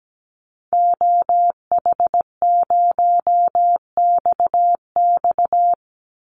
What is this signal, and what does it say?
Morse code, heard as a keyed tone, about 17 wpm, 700 Hz. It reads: OH0XX
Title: Call Signs Repeated 3x - 17wpm